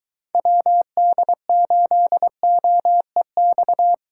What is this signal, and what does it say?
Morse code, heard as a keyed tone, about 21 wpm, 700 Hz. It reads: WD8OEX